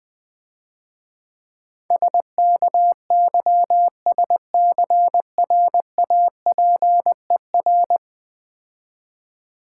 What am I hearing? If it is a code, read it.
Morse code, heard as a keyed tone, about 20 wpm, 700 Hz. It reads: SKYSCRAPER
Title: Easy Words - Speed-Racing with Hints - 20wpm